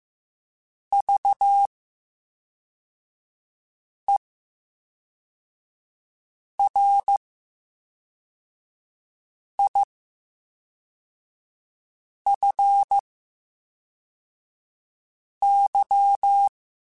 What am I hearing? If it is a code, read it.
Morse code, heard as a keyed tone, about 15 wpm, spaced as for 3 wpm, 800 Hz. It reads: VERIFY
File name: Special INGs / Special INGs-mp3s / VerySlowToFaster Ing @10